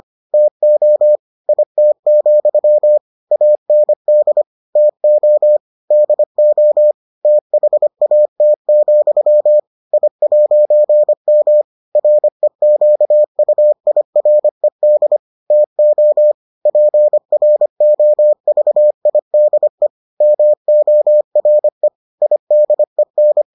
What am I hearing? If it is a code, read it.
Morse code, heard as a keyed tone, about 25 wpm, 600 Hz. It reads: TO IT, AND TO DO THAT, I'M REQUIRED TO PROVIDE MORE IDEN